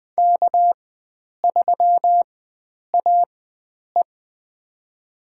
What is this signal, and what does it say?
Morse code, heard as a keyed tone, about 20 wpm, spaced as for 9 wpm, 700 Hz. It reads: K3AE
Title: USA Call Signs - 4 Characters - 4x Character Spacing - 20wpm